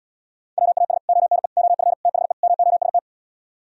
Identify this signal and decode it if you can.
Morse code, heard as a keyed tone, about 77 wpm, 700 Hz. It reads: WAS ONE OF THE MOST